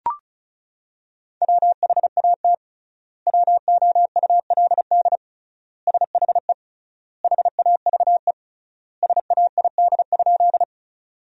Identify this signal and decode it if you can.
Morse code, heard as a keyed tone, about 34 wpm, 700 Hz. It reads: WHAT WOULD SHE HAVE SAID?
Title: 3x Word-Spacing - Sentences from Top 100 Words - 35wpm